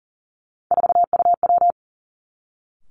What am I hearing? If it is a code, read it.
Morse code, heard as a keyed tone, about 40 wpm, 700 Hz. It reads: 4UW
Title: Three Challenging Letter-Number Combinations 40wpm